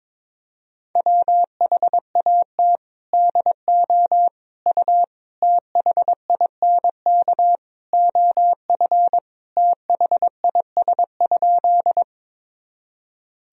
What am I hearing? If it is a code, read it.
Morse code, heard as a keyed tone, about 22 wpm, 700 Hz. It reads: WHAT DO U THINK OF THIS?
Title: Sentences from Top 200 Words and Abbreviations 22wpm